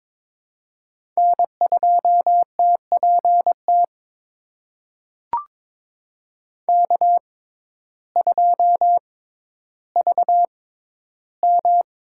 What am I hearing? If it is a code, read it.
Morse code, heard as a keyed tone, about 22 wpm, spaced as for 12 wpm, 700 Hz. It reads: N2TPT K2VM
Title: USA Call Signs - 6x Character Spacing - 22wpm